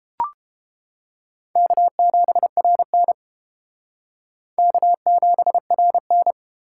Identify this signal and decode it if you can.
Morse code, heard as a keyed tone, about 30 wpm, 700 Hz. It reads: K7RN K7RN